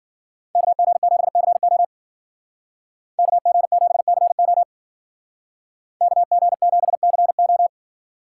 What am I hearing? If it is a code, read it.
Morse code, heard as a keyed tone, about 48 wpm, 700 Hz. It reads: KG7CO KG7CO KG7CO
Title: Speed Racing - USA 2x2 Call Signs 40wpm